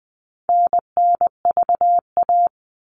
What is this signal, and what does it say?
Morse code, heard as a keyed tone, about 20 wpm, 700 Hz. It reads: NNVA
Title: All Cut Numbers - 4-Digit 20wpm